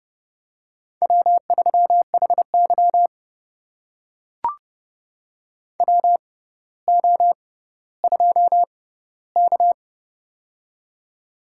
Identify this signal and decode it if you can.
Morse code, heard as a keyed tone, about 31 wpm, spaced as for 15 wpm, 700 Hz. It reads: W3HY WO2K
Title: USA Call Signs - 4 Characters - 6x Character Spacing - 30wpm